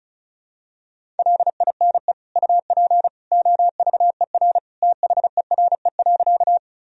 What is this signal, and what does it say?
Morse code, heard as a keyed tone, about 34 wpm, 700 Hz. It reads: LINE UP OVER THERE.
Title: Sentences from Top 1000 Words - Review Part 1 - 35wpm